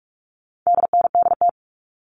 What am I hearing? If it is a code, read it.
Morse code, heard as a keyed tone, about 44 wpm, 700 Hz. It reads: DNDT
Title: All Cut Numbers - 4-Digit 45wpm